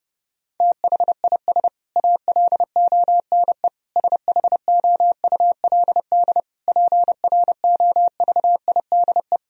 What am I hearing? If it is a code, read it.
Morse code, heard as a keyed tone, about 30 wpm, 700 Hz. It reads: THIS ALONE SHOULD PROVIDE
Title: Sentences from Top 1000 Words - Review Part 4 - 30wpm